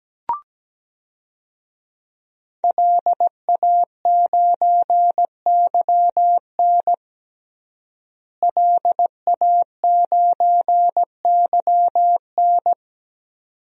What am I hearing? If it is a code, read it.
Morse code, heard as a keyed tone, about 17 wpm, 700 Hz. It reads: LA9YN LA9YN
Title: DXCC-100 Call Signs - 5 Characters - Repeated 2x - 17wpm